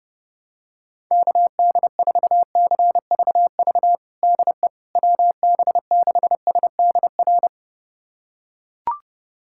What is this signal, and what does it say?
Morse code, heard as a keyed tone, about 32 wpm, 700 Hz. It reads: KD4CVV DE WB6SDR
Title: USA Callsign DE Callsign - 6 Characters - 30wpm